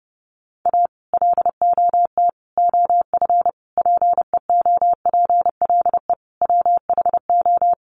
Morse code, heard as a keyed tone, about 30 wpm, 700 Hz. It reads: A LOT OF PEOPLE WHO
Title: Sets of 5 Words 30wpm